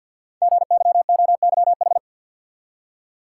Morse code, heard as a keyed tone, about 53 wpm, 700 Hz. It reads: GYOYH
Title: Code Groups - 5 Letter:Numbers - 50wpm